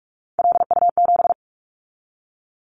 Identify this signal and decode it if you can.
Morse code, heard as a keyed tone, about 46 wpm, 700 Hz. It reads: LU7
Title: Callsign Prefixes for Top 100 DXCC - 45wpm